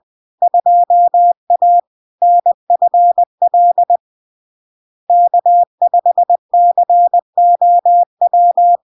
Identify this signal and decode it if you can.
Morse code, heard as a keyed tone, about 20 wpm, 700 Hz. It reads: 2A NFL K5COW
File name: LARK-FD-2022-Log-in-CW-20wpm-700-Hz